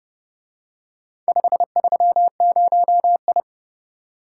Morse code, heard as a keyed tone, about 29 wpm, 700 Hz. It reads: 530I